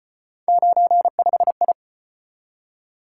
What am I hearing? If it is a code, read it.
Morse code, heard as a keyed tone, about 34 wpm, 700 Hz. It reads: 95I